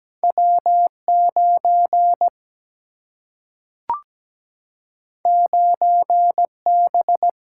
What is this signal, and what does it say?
Morse code, heard as a keyed tone, about 17 wpm, 700 Hz. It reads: W9 9B
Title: Single-Letter Single-Number 17wpm